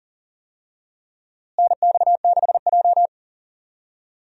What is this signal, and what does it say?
Morse code, heard as a keyed tone, about 40 wpm, 700 Hz. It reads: NX6J
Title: USA Call Signs - 4 Characters - Repeated 3x - 40wpm